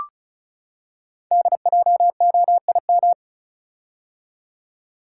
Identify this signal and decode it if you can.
Morse code, heard as a keyed tone, about 36 wpm, 690 Hz. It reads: DJOIM
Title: Code Groups - 5 Letters - 35wpm